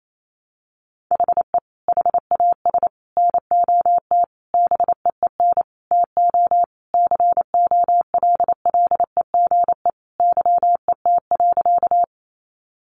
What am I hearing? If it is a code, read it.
Morse code, heard as a keyed tone, about 28 wpm, 700 Hz. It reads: HE HAS NOT BEEN TO COLLEGE YET.
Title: Sentences from Top 1300 Words 28wpm